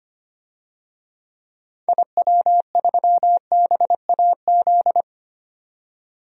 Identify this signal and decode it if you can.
Morse code, heard as a keyed tone, about 25 wpm, 700 Hz. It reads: IW3BAZ
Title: DXCC-100 Call Signs - 6 Characters - 25wpm